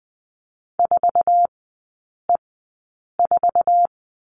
Morse code, heard as a keyed tone, about 20 wpm, spaced as for 8 wpm, 700 Hz. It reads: 4E4
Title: 121 Lesson 20e - Number 3 - Three Characters 20wpm